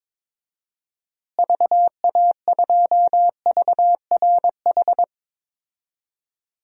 Morse code, heard as a keyed tone, about 22 wpm, 700 Hz. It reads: VA2VRH
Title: DXCC-100 Call Signs - 6 Characters - Repeated 3x - 22wpm